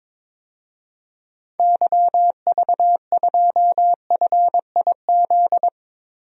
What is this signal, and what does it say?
Morse code, heard as a keyed tone, about 22 wpm, 700 Hz. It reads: YV2FIZ
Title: DXCC-100 Call Signs - 6 Characters - Repeated 2x - 22wpm